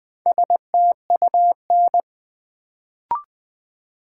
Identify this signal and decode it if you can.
Morse code, heard as a keyed tone, about 20 wpm, 700 Hz. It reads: STUN
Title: Build A Word - 1 Letter At A Time - 4-Letter Words - 20wpm